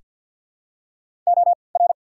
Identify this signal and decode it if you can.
Morse code, heard as a keyed tone, about 48 wpm, spaced as for 28 wpm, 700 Hz. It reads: OR
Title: 3x Character Spacing - Top 100 Words - 50wpm